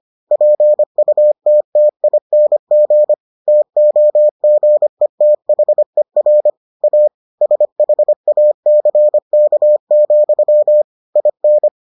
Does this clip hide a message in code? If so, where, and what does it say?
Yes: Morse code from 0.3 to 11.7 s, heard as a keyed tone, about 25 wpm, 600 Hz. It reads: PUTTING TOGETHER A SHACK, IN